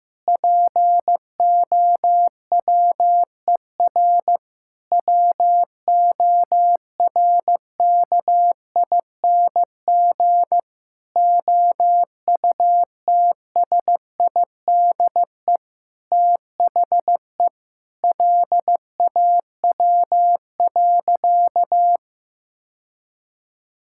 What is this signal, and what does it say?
Morse code, heard as a keyed tone, about 16 wpm, 700 Hz. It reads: POWER WORKING OUTSIDE THE LAW.
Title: Sentences from Top 1000 Words - Review Part 6 - 15wpm